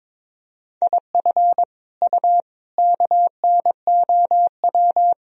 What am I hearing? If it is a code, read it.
Morse code, heard as a keyed tone, about 22 wpm, 700 Hz. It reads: IF U KNOW
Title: Sentences from Top 400 Words and Abbreviations 22wpm